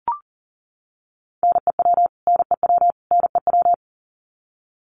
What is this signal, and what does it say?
Morse code, heard as a keyed tone, about 40 wpm, 700 Hz. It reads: NEW NEW NEW
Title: Top 500 Words Repeated 40wpm